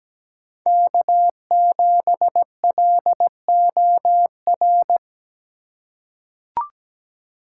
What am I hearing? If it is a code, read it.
Morse code, heard as a keyed tone, about 17 wpm, 700 Hz. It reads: K7LOR
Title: USA Call Signs - 2x Character Spacing - 17wpm